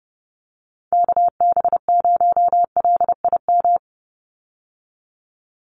Morse code, heard as a keyed tone, about 30 wpm, 700 Hz. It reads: KB0LIM
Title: USA Call Signs 30wpm